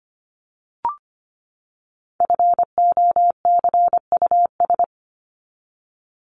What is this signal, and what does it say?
Morse code, heard as a keyed tone, about 25 wpm, 700 Hz. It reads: FOCUS
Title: Top 501-750 Words 25wpm